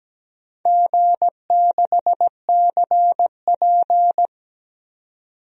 Morse code, heard as a keyed tone, about 17 wpm, 700 Hz. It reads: G6CP
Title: DXCC-100 Call Signs - 4 Characters - 17wpm